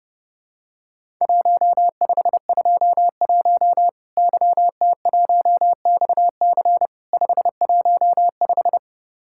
Morse code, heard as a keyed tone, about 30 wpm, 700 Hz. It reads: 1521 YT1XC 515
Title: Worked All Europe DX Contest - QTC Exchange - 30wpm